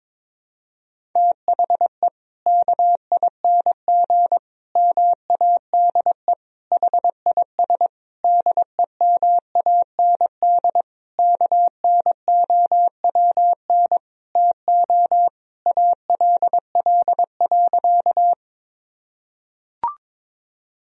Morse code, heard as a keyed tone, about 21 wpm, 700 Hz. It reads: THE KING MADE HIS DEMAND KNOWN TO ALL.
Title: Sentences from Top 1000 Words - Review Part 4 - 22wpm